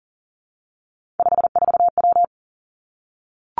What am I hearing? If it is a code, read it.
Morse code, heard as a keyed tone, about 41 wpm, 690 Hz. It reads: 54W
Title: Three Challenging Letter-Number Combinations 40wpm